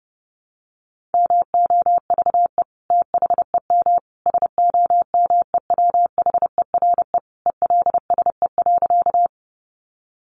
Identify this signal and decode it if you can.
Morse code, heard as a keyed tone, about 30 wpm, 700 Hz. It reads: MOVE THEM SOMEWHERE ELSE.
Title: Sentences from Top 1400 Words 30wpm